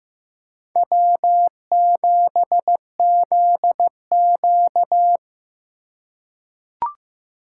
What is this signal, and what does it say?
Morse code, heard as a keyed tone, about 15 wpm, 700 Hz. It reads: W7ZQ
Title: USA Call Signs - 4 Characters - 2x Character Spacing - 15wpm